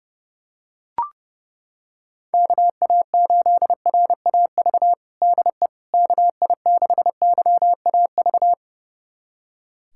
Morse code, heard as a keyed tone, about 30 wpm, 700 Hz. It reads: KA8RAV DE KI6YAV
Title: USA Callsign DE Callsign - 6 Characters - 30wpm